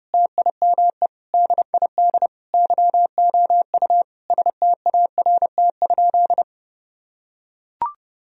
Morse code, heard as a keyed tone, about 30 wpm, 700 Hz. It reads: TIME DID YOU START?
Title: Sentences from Top 1000 Words - Review Part 1 - 30wpm